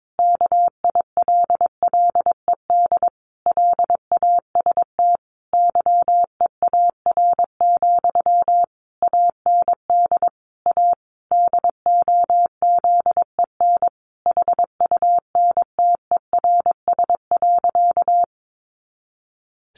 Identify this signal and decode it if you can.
Morse code, heard as a keyed tone, about 22 wpm, 700 Hz. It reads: KILLED LAST YEAR, AND A DOZEN HUNTERS.
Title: A Sound Of Thunder 22wpm